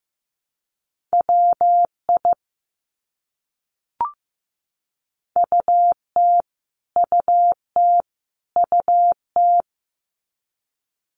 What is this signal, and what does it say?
Morse code, heard as a keyed tone, about 15 wpm, 700 Hz. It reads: WI UT UT UT
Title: US States Repeated 3x 15wpm